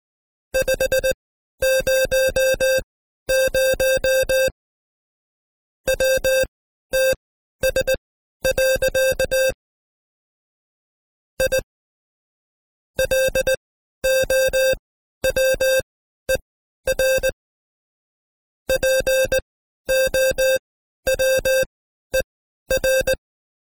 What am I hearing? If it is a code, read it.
Morse code, heard as a keyed tone, about 18 wpm, spaced as for 12 wpm, 530 Hz. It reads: 500 WTS. I LOWER POWER
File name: Code Quick / AudioDownld2 / Seg132-Stepper5 11WPM